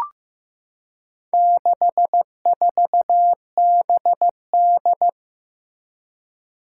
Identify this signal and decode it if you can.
Morse code, heard as a keyed tone, about 15 wpm, 700 Hz. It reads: 64BD
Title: Four Challenging Letter-Number Combinations 15wpm